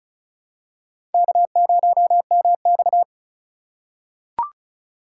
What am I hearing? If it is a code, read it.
Morse code, heard as a keyed tone, about 36 wpm, 700 Hz. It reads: K0MX
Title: USA Call Signs - 2x Character Spacing - 35wpm